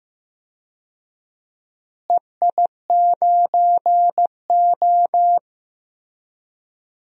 Morse code, heard as a keyed tone, about 15 wpm, 700 Hz. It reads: EI9O
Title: DXCC-100 Call Signs - 4 Characters - Repeated 2x - 15wpm